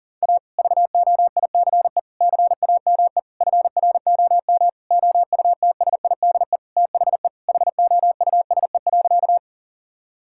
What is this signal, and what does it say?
Morse code, heard as a keyed tone, about 40 wpm, 700 Hz. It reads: A VOICE CAME FROM OUTSIDE THE HOUSE.
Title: Sentences from Top 1000 Words - Review Part 6 - 40wpm